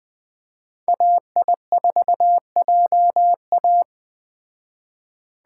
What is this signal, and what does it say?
Morse code, heard as a keyed tone, about 20 wpm, 700 Hz. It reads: AI4JA